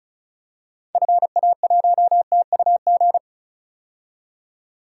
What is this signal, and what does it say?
Morse code, heard as a keyed tone, about 36 wpm, 700 Hz. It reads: FA1TUG
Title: DXCC-100 Call Signs - 6 Characters - 35wpm